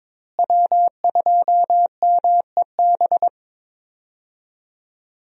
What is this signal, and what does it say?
Morse code, heard as a keyed tone, about 22 wpm, 700 Hz. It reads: W2MEB
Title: USA Call Signs - 5 Characters - Repeated 2x - 22wpm